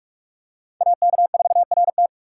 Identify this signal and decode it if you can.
Morse code, heard as a keyed tone, about 46 wpm, 700 Hz. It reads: AK4RT
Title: USA Call Signs - 2x Character Spacing - 45wpm